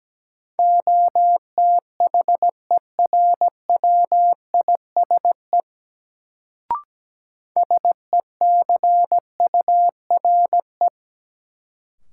Morse code, heard as a keyed tone, about 17 wpm, 700 Hz. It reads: OTHERWISE SECURE
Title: Sentences from Top 1200 Words - New Word Drill - 17wpm